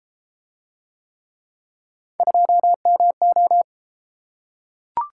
Morse code, heard as a keyed tone, about 31 wpm, 700 Hz. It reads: 2MO